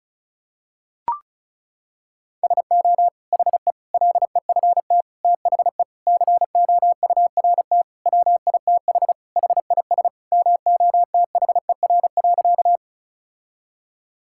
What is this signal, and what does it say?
Morse code, heard as a keyed tone, about 36 wpm, 700 Hz. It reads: SO HE LEFT THE COURT WITH HIS MOTHER.
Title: Sentences from Top 1600 Words 35wpm